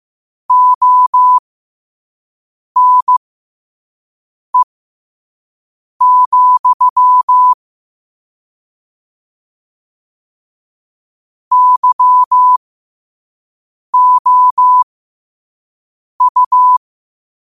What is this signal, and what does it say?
Morse code, heard as a keyed tone, about 15 wpm, spaced as for 5 wpm, 1000 Hz. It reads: ONE, YOU
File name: qotdfiles / 5wpm